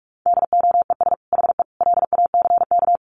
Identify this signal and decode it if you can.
Morse code, heard as a keyed tone, about 44 wpm, 700 Hz. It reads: DOES HE LACK
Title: Sentences from Top 1100 Words 45wpm